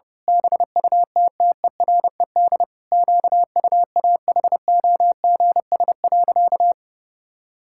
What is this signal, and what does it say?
Morse code, heard as a keyed tone, about 32 wpm, 700 Hz. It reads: BUTTERED QUAHOGS.
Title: Pangram Challenge v2 - 30wpm